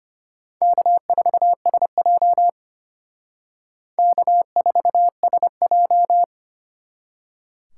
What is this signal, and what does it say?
Morse code, heard as a keyed tone, about 32 wpm, 700 Hz. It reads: K4SJ K4SJ